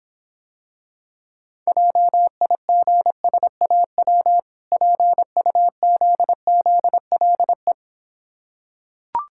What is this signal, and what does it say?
Morse code, heard as a keyed tone, about 26 wpm, 700 Hz. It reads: JIGSAW PUZZLE